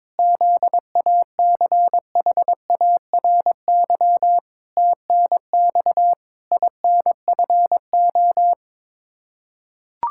Morse code, heard as a keyed tone, about 22 wpm, 700 Hz. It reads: ZACHARY TNX INFO
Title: Practice QSO - Exchange 3 - More Info - 22wpm